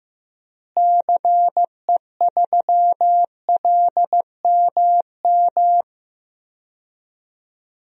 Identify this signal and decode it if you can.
Morse code, heard as a keyed tone, about 15 wpm, 700 Hz. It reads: CE3LMM
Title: DXCC-100 Call Signs - 6 Characters - Repeated 3x - 15wpm